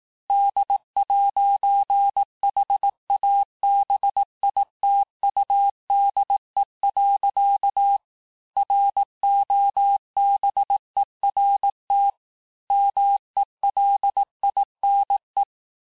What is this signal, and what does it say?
Morse code, heard as a keyed tone, about 18 wpm, 800 Hz. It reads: D'HABITUDE. ROBERT MELINE
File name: cours-du-lundi files / cours-1852-FIN-du-QSO-vitesse-17MPM